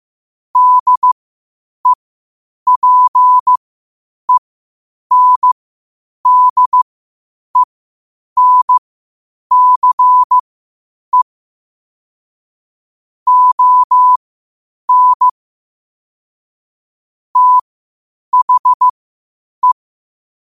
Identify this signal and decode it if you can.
Morse code, heard as a keyed tone, about 15 wpm, spaced as for 8 wpm, 1000 Hz. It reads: DEPENDENCE ON THE
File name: qotdfiles / 8wpm